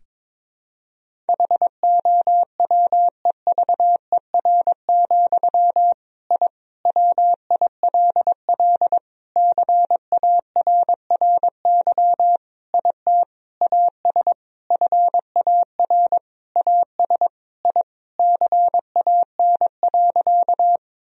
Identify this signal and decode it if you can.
Morse code, heard as a keyed tone, about 22 wpm, 700 Hz. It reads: HOWEVER, I WILL CARRY IT AS FAR AS I CAN.
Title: Sentences from Top 1000 Words - Review Part 1 - 22wpm